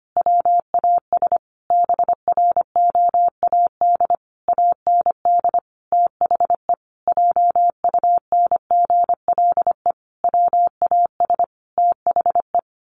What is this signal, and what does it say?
Morse code, heard as a keyed tone, about 25 wpm, 700 Hz. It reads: WAS BROAD AND THE JUNGLE WAS THE
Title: A Sound Of Thunder 25wpm